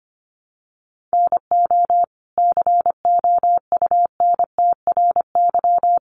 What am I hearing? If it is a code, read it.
Morse code, heard as a keyed tone, about 25 wpm, 700 Hz. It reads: NO COUNTRY
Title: Sentences from Top 800 Words 25wpm